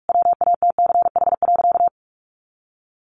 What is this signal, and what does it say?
Morse code, heard as a keyed tone, about 44 wpm, 700 Hz. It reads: WATCH.
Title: A Sound Of Thunder 45wpm